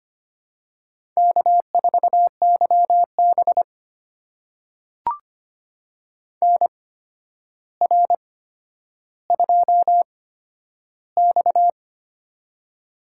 Morse code, heard as a keyed tone, about 25 wpm, spaced as for 11 wpm, 700 Hz. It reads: K4YB NR2X
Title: USA Call Signs - 4 Characters - 8x Character Spacing - 25wpm